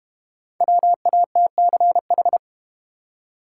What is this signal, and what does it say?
Morse code, heard as a keyed tone, about 30 wpm, 700 Hz. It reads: WATCH
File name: Easy Words - Speed-Racing with Hints - 25wpm